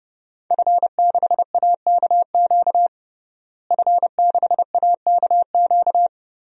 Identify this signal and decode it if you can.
Morse code, heard as a keyed tone, about 31 wpm, 700 Hz. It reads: F6AKQ F6AKQ
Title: DXCC-100 Call Signs - 5 Characters - Repeated 3x - 30wpm